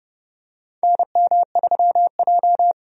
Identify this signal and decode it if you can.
Morse code, heard as a keyed tone, about 31 wpm, 700 Hz. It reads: NM3J
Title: USA Call Signs - 4 Characters - 30wpm